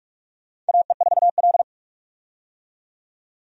Morse code, heard as a keyed tone, about 46 wpm, 700 Hz. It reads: AE4L
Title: USA Call Signs - 4 Characters - 45wpm